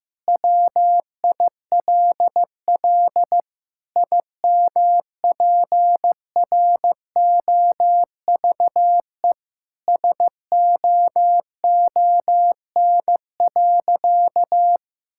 Morse code, heard as a keyed tone, about 15 wpm, 700 Hz. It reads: WILL IMPROVE SOON.
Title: Sentences from Top 1000 Words - Review Part 5 - 15wpm